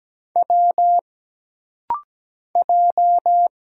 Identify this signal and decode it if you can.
Morse code, heard as a keyed tone, about 17 wpm, 700 Hz. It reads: W J
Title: Single Letters - Rapid-Fire 17wpm